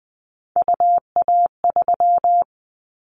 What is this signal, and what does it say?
Morse code, heard as a keyed tone, about 20 wpm, 700 Hz. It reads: UA3
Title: Callsign Prefixes for Top 25 DXCC - 20wpm